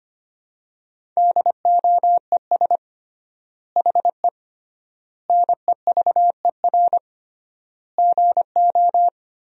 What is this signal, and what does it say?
Morse code, heard as a keyed tone, about 25 wpm, 700 Hz. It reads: DOES HE NEVER GO
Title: 3x Word-Spacing - Sentences from Top 300 Words - 25wpm